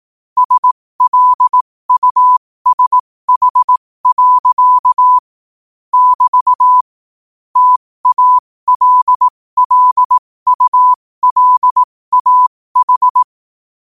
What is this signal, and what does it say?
Morse code, heard as a keyed tone, about 18 wpm, spaced as for 15 wpm, 1000 Hz. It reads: SLUSH. <BT> TALLULAH
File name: qotdfiles / 15wpm